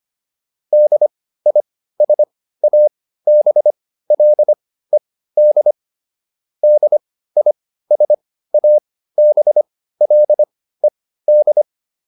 Morse code, heard as a keyed tone, about 25 wpm, spaced as for 15 wpm, 600 Hz. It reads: DISABLED DISABLED